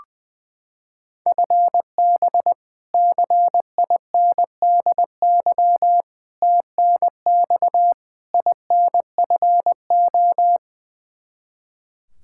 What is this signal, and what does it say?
Morse code, heard as a keyed tone, about 20 wpm, 700 Hz. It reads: FB CINDY TNX INFO
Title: Practice QSO - Exchange 3 - More Info - 20wpm